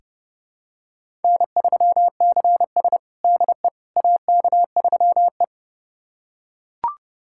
Morse code, heard as a keyed tone, about 30 wpm, 700 Hz. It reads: N3CS DE AK3E
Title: USA Callsign DE Callsign - 4 Characters - 30wpm